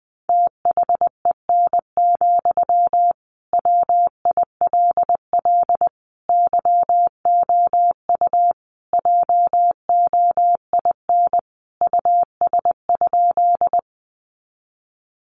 Morse code, heard as a keyed tone, about 20 wpm, 700 Hz. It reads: THEN, WILL YOU JOIN US?
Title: Sentences from Top 1400 Words 20wpm